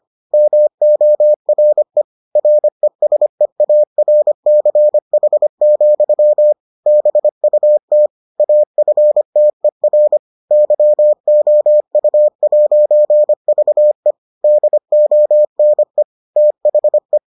Morse code, heard as a keyed tone, about 25 wpm, 600 Hz. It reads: MORE RESEARCH, BUT AFTER YOU'VE DONE THE